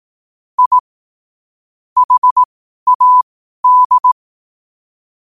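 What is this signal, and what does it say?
Morse code, heard as a keyed tone, about 18 wpm, spaced as for 12 wpm, 1000 Hz. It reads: I HAD
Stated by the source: Morse code sent at 12 words per minute.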